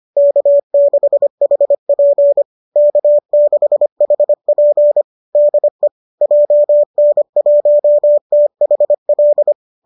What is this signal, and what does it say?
Morse code, heard as a keyed tone, about 25 wpm, 570 Hz. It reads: K6HP K6HP DE JN1THL
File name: qso205 25